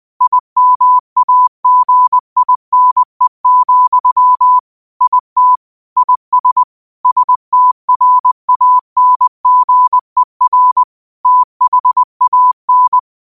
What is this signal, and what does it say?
Morse code, heard as a keyed tone, about 20 wpm, 1000 Hz. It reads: IMAGINE, IT IS STRANGER THAN
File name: qotdfiles / 20wpm